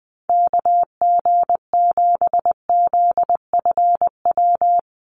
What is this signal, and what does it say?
Morse code, heard as a keyed tone, about 20 wpm, 700 Hz. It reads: KG7ZFW
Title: USA Call Signs Repeated 2x 20wpm